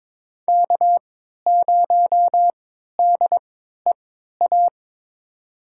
Morse code, heard as a keyed tone, about 22 wpm, spaced as for 12 wpm, 700 Hz. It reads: K0DEA
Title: USA Call Signs - 5 Characters - 3x Character Spacing - 22wpm